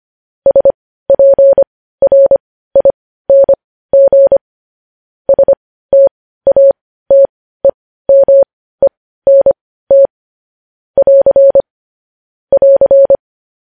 Morse code, heard as a keyed tone, about 25 wpm, spaced as for 15 wpm, 550 Hz. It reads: SPRING STATEMENT <AR> <AR>
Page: News Headlines in Morse Code at 15 WPM